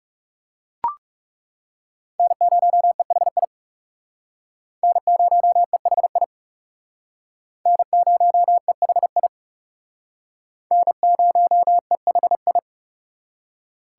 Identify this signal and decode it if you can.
Morse code, heard as a keyed tone, about 37 wpm, 700 Hz. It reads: N0EHI N0EHI N0EHI N0EHI
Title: Speed Racing - USA 1x3 Call Signs 30wpm